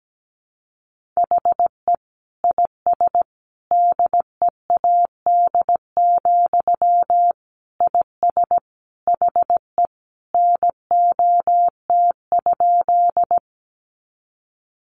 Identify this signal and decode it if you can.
Morse code, heard as a keyed tone, about 17 wpm, 700 Hz. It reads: HE IS DEAD, IS HE NOT?
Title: Sentences from Top 1500 Words 17wpm